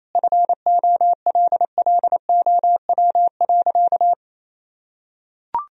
Morse code, heard as a keyed tone, about 28 wpm, 700 Hz. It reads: FOLLOW.
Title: Sentences from Top 1000 Words - Review Part 3 - 28wpm